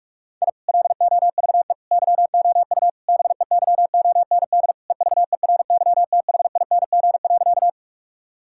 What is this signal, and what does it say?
Morse code, heard as a keyed tone, about 46 wpm, 700 Hz. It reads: I LOVE YOU BEYOND EVERYTHING.
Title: Sentences from Top 1000 Words - Review Part 6 - 45wpm